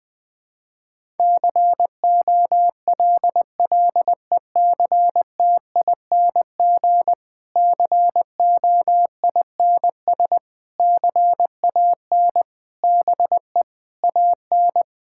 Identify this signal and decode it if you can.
Morse code, heard as a keyed tone, about 20 wpm, 700 Hz. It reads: COLLECTING COINS CAN BE AN
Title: Difficult Sentences with Hints - 20wpm